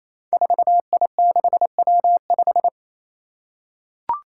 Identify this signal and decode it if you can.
Morse code, heard as a keyed tone, about 28 wpm, 700 Hz. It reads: 4I6W5
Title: Five Challenging Letter-Number Combinations 28wpm